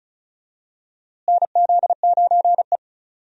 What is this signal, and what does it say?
Morse code, heard as a keyed tone, about 36 wpm, 700 Hz. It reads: NZ9E
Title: USA Call Signs - 4 Characters - Repeated 3x - 35wpm